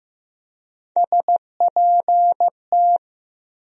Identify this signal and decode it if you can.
Morse code, heard as a keyed tone, about 15 wpm, 700 Hz. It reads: SPT
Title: USA Call Sign Postfixes - 3 Letters 15wpm